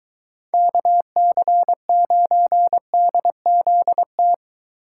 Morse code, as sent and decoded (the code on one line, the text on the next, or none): -.- -.-. ----. -.. --.. -
KC9DZT